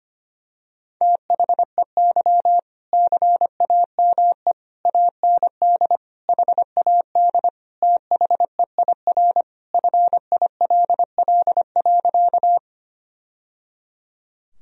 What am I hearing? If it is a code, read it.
Morse code, heard as a keyed tone, about 25 wpm, 700 Hz. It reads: THEY CAME AND HAD THEIR FILL.
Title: Sentences from Top 1000 Words - Review Part 3 - 25wpm